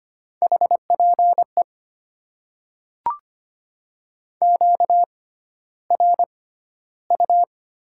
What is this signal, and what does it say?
Morse code, heard as a keyed tone, about 25 wpm, spaced as for 13 wpm, 700 Hz. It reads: HPE QRU
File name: Top 90 QSO Elements - 6x Character Spacing - 25wpm